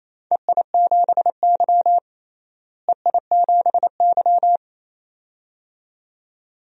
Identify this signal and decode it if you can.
Morse code, heard as a keyed tone, about 28 wpm, 700 Hz. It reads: EI7Y EI7Y
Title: DXCC-100 Call Signs - 4 Characters - Repeated 2x - 28wpm